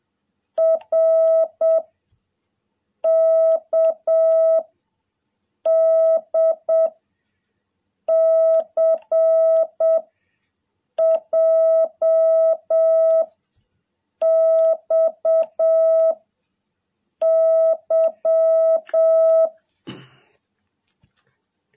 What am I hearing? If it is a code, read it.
Morse code, heard as a keyed tone, about 7 wpm, 660 Hz. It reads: R K D C J X Y
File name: jh8cbh / 2021cw